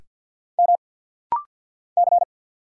Morse code, heard as a keyed tone, about 49 wpm, 700 Hz. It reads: M C
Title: Single Letter-Number Alt - Rapid-Fire 50wpm